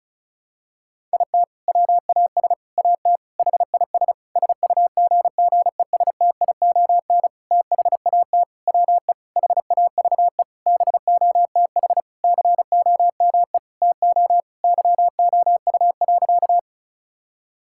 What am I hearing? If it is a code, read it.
Morse code, heard as a keyed tone, about 36 wpm, 700 Hz. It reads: IT WAS AT HIS SUGGESTION THAT WE HAVE BOTH COME TO YOU.
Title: Sentences from Top 1600 Words 35wpm